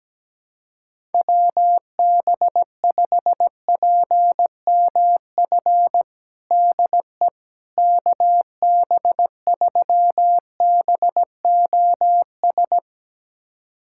Morse code, heard as a keyed tone, about 17 wpm, 700 Hz. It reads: WB5PMF DE KB3BOS
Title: USA Callsign DE Callsign 17wpm